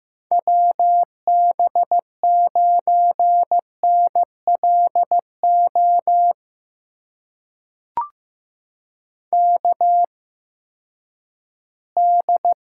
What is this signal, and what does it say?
Morse code, heard as a keyed tone, about 15 wpm, spaced as for 10 wpm, 700 Hz. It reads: WB9NLO KD
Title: USA Call Signs - 8x Character Spacing - 15wpm